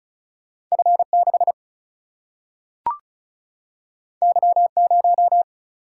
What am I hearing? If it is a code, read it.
Morse code, heard as a keyed tone, about 36 wpm, 700 Hz. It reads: F 6 Y 0
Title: Single-Letter Single-Number 35wpm